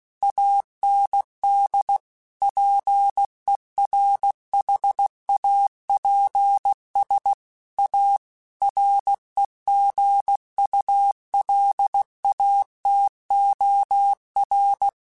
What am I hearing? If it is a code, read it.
Morse code, heard as a keyed tone, about 16 wpm, 800 Hz. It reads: AND PERHAPS A REGULATOR
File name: Technical Subjects / Technical Subjects-mp3s / Sw Pwr Sup @15